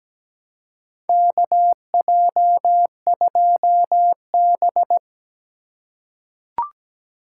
Morse code, heard as a keyed tone, about 17 wpm, 700 Hz. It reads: KJ2B
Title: USA Call Signs - 4 Characters - 2x Character Spacing - 17wpm